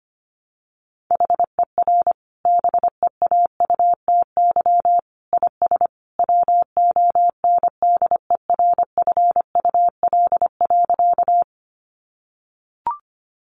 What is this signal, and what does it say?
Morse code, heard as a keyed tone, about 25 wpm, 700 Hz. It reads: HER BEAUTY IS WONDERFUL.
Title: Sentences from Top 1100 Words 25wpm